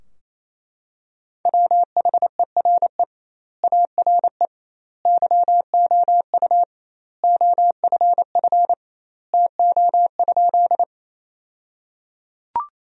Morse code, heard as a keyed tone, about 27 wpm, 700 Hz. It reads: WHERE ARE YOU OFF TO?
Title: 2x Word-Spacing - Sentences from Top 200 Words - 28wpm